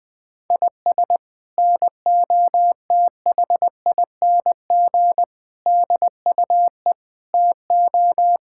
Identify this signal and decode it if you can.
Morse code, heard as a keyed tone, about 20 wpm, 700 Hz. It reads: IS NOTHING DUE TO
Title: Sentences from Top 1000 Words - Review Part 6 - 20wpm